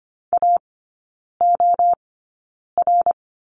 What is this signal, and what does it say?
Morse code, heard as a keyed tone, about 25 wpm, spaced as for 9 wpm, 700 Hz. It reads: AOR
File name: course / 25:10wpm / 093 Lesson 16e - Letter c - Three Characters 25wpm